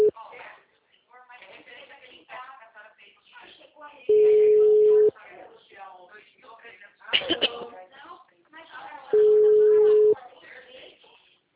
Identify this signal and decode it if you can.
Ringback tone (Europe, 425 Hz), with speech.